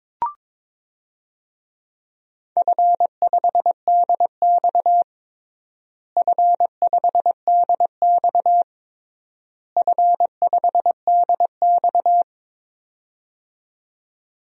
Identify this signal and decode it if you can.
Morse code, heard as a keyed tone, about 22 wpm, 700 Hz. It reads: F5DX F5DX F5DX
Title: DXCC-100 Call Signs - 4 Characters - Repeated 3x - 22wpm